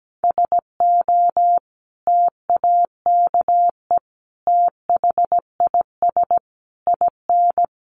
Morse code, heard as a keyed tone, about 17 wpm, 700 Hz. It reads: SO TAKE THIS IN
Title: Sentences from Top 1500 Words 17wpm